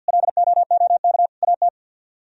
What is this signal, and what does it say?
Morse code, heard as a keyed tone, about 60 wpm, 700 Hz. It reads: LOOK AT
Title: Speed-Racing - Sets of 3 Words 50wpm